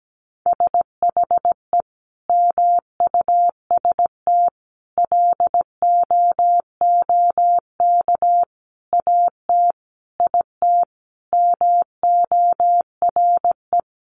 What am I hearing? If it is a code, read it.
Morse code, heard as a keyed tone, about 17 wpm, 700 Hz. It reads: SHE MUST LOOK AT IT MORE
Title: Sentences from Top 1400 Words 17wpm